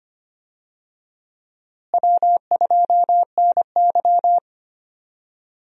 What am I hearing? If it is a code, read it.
Morse code, heard as a keyed tone, about 25 wpm, 700 Hz. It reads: W2NY
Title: USA Call Signs - 4 Characters - Repeated 3x - 25wpm